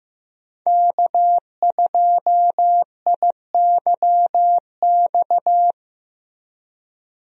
Morse code, heard as a keyed tone, about 15 wpm, 700 Hz. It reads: K2IYX
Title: USA Call Signs - 5 Characters - Repeated 2x - 15wpm